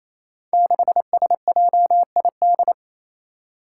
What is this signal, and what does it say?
Morse code, heard as a keyed tone, about 29 wpm, 700 Hz. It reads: 6SJID
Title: Five Challenging Letter-Number Combinations 28wpm